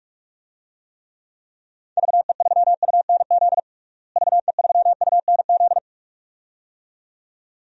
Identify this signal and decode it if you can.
Morse code, heard as a keyed tone, about 42 wpm, 700 Hz. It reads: VE3UNZ VE3UNZ
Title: DXCC-100 Call Signs - 6 Characters - Repeated 2x - 45wpm